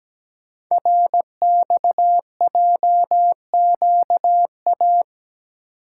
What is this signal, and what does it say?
Morse code, heard as a keyed tone, about 17 wpm, 700 Hz. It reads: RXJQA
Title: Code Groups - 5 Letters - 17wpm